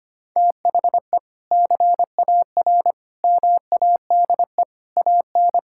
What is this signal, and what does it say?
Morse code, heard as a keyed tone, about 25 wpm, 700 Hz. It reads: THE CAR MADE AN
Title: Sentences from Top 1000 Words - Review Part 6 - 25wpm